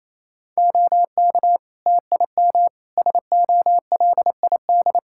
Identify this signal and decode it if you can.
Morse code, heard as a keyed tone, about 28 wpm, 700 Hz. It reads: OK TIM SOLID
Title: Practice QSO - Exchange 3 - More Info - 28wpm